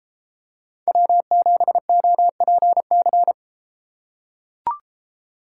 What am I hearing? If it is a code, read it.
Morse code, heard as a keyed tone, about 33 wpm, 700 Hz. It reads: W7OPC